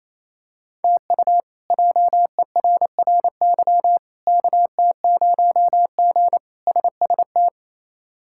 Morse code, heard as a keyed tone, about 28 wpm, 700 Hz. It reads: TU JERRY KT0G SST
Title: K1USN SST QSO Practice - 28wpm